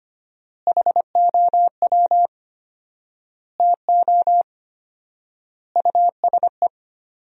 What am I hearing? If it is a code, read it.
Morse code, heard as a keyed tone, about 25 wpm, 700 Hz. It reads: HOW TO USE
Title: 4x Word-Spacing - Sentences from Top 500 Words - 25wpm